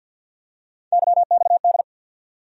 Morse code, heard as a keyed tone, about 52 wpm, 700 Hz. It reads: YXD
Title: USA Call Sign Postfixes - 3 Letters 50wpm